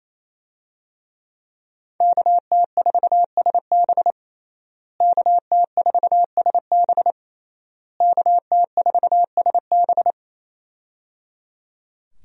Morse code, heard as a keyed tone, about 28 wpm, 700 Hz. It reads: KT4SB KT4SB KT4SB
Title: USA Call Signs - 5 Characters - Repeated 3x - 28wpm